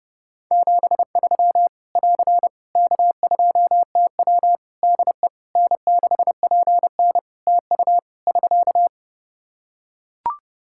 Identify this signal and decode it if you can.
Morse code, heard as a keyed tone, about 30 wpm, 700 Hz. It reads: 73 <AR> K2TW DE N6PN TU <SK>
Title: Practice QSO - Exchange 4 - Ending - 30wpm